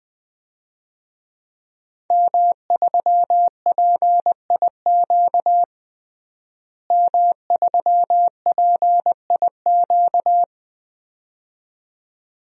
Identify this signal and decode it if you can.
Morse code, heard as a keyed tone, about 20 wpm, 700 Hz. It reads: M3PIQ M3PIQ
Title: DXCC-100 Call Signs - 5 Characters - Repeated 2x - 20wpm